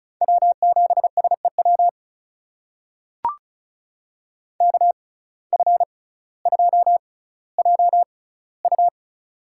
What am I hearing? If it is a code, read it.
Morse code, heard as a keyed tone, about 36 wpm, spaced as for 18 wpm, 700 Hz. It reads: W7SEW KF2JU
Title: USA Call Signs - 5 Characters - 6x Character Spacing - 35wpm